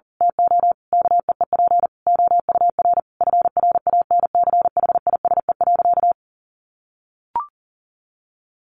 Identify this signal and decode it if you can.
Morse code, heard as a keyed tone, about 37 wpm, 700 Hz. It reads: TO KEEP OUR FRANCHISE.
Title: A Sound Of Thunder 40wpm